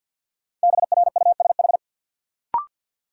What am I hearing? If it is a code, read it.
Morse code, heard as a keyed tone, about 53 wpm, 700 Hz. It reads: BRUSH
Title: Sentences from Top 1100 Words - New Word Drill Alt - 50wpm